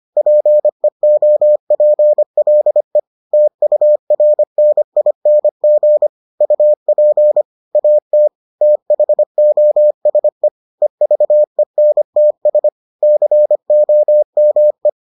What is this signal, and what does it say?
Morse code, heard as a keyed tone, about 25 wpm, 600 Hz. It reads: PEOPLE TURNING UP AT THOSE EVENTS COME